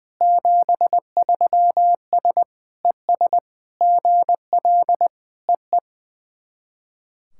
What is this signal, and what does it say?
Morse code, heard as a keyed tone, about 20 wpm, 700 Hz. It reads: 73S ES GL EE
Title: SOTA QSO Practice 20wpm